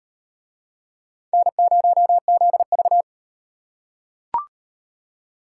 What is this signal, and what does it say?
Morse code, heard as a keyed tone, about 38 wpm, 700 Hz. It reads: N0ZV